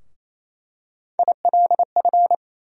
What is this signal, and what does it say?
Morse code, heard as a keyed tone, about 28 wpm, 700 Hz. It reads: ILF
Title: USA Call Sign Postfixes - 3 Letters 28wpm